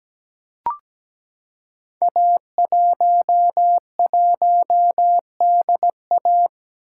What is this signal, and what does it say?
Morse code, heard as a keyed tone, about 17 wpm, 700 Hz. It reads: A11DA
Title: Code Groups - 5 Letter:Numbers - 17wpm